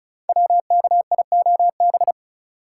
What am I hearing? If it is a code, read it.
Morse code, heard as a keyed tone, about 36 wpm, 700 Hz. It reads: WKIOB
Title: Code Groups - 5 Letters - 35wpm